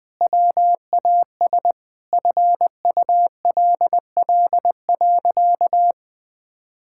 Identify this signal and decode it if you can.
Morse code, heard as a keyed tone, about 21 wpm, 700 Hz. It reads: WAS FULL.
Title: Sentences from Top 1600 Words 20wpm